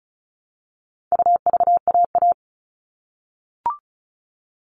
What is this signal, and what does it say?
Morse code, heard as a keyed tone, about 34 wpm, 700 Hz. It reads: UVAA
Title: All Cut Numbers - 4-Digit 35wpm